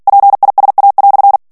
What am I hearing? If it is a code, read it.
Morse code, heard as a keyed tone, about 48 wpm, 800 Hz. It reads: PISA"
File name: CW / FICHIERS-MP3 / FRANCAIS TEXTE CLAIR 1 VITESSE 45 MOTS MINUTES